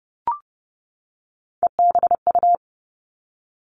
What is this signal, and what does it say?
Morse code, heard as a keyed tone, about 30 wpm, 700 Hz. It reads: EBU
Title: Three Challenging Letter-Number Combinations 30wpm